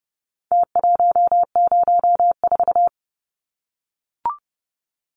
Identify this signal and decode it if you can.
Morse code, heard as a keyed tone, about 30 wpm, 700 Hz. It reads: T104
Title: Numbers - Contest Style - 4-Digit 30wpm